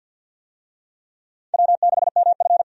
Morse code, heard as a keyed tone, about 52 wpm, 700 Hz. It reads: W6MF
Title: USA Call Signs - 4 Characters - 50wpm